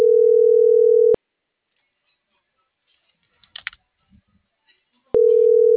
Ringback tone (North America), 440 + 480 Hz.